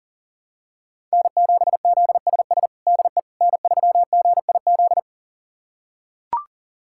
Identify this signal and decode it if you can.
Morse code, heard as a keyed tone, about 40 wpm, 700 Hz. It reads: N7ZSS DE N3GIZ
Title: USA Callsign DE Callsign - 5 Characters - 40wpm